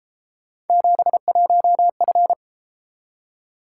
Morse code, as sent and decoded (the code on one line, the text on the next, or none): --... .---- ..-.
71F